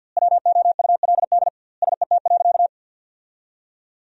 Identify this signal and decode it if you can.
Morse code, heard as a keyed tone, about 50 wpm, 700 Hz. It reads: WOULD SET.
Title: 2x Word-Spacing - Sentences from Top 400 Words - 50wpm